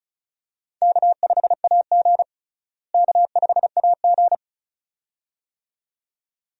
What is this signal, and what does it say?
Morse code, heard as a keyed tone, about 36 wpm, 700 Hz. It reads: K5AG K5AG
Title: USA Call Signs - 4 Characters - Repeated 2x - 35wpm